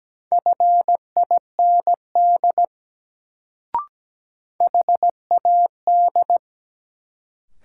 Morse code, heard as a keyed tone, about 17 wpm, 700 Hz. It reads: FIND HAD
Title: Sentences from Top 100 Words - New Word Drill - 17wpm